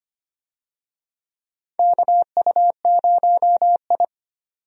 Morse code, heard as a keyed tone, about 25 wpm, 700 Hz. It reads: KU0I
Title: USA Call Signs - 4 Characters - 25wpm